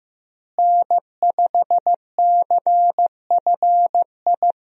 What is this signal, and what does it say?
Morse code, heard as a keyed tone, about 16 wpm, 700 Hz. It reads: N5CFI
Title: USA Call Signs - 5 Characters - Repeated 3x - 15wpm